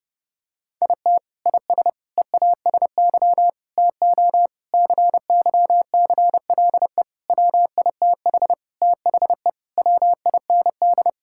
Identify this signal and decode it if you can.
Morse code, heard as a keyed tone, about 30 wpm, 700 Hz. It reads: IT IS EASY TO CYCLE WITH THE WIND
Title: Sentences from Top 1000 Words - Review Part 6 - 30wpm